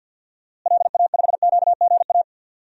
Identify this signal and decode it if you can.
Morse code, heard as a keyed tone, about 51 wpm, 700 Hz. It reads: LA5QGA
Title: DXCC-100 Call Signs - 6 Characters - Repeated 2x - 50wpm